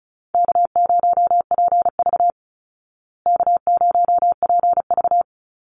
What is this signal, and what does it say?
Morse code, heard as a keyed tone, about 34 wpm, 700 Hz. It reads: K0PV K0PV
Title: Easy Call Signs Repeated 2x - 35wpm